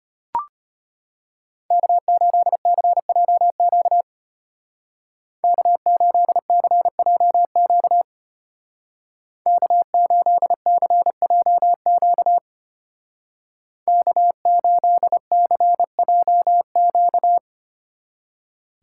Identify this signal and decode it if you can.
Morse code, heard as a keyed tone, about 32 wpm, 700 Hz. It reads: K8CJQ K8CJQ K8CJQ K8CJQ